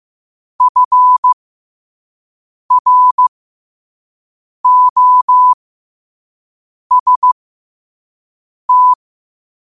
Morse code, heard as a keyed tone, about 15 wpm, spaced as for 5 wpm, 1000 Hz. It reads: FROST
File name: qotdfiles / 5wpm